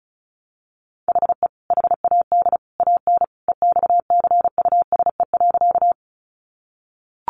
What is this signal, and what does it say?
Morse code, heard as a keyed tone, about 34 wpm, 700 Hz. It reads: HE HAD AN EXCUSE.
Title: Sentences from Top 800 Words 35wpm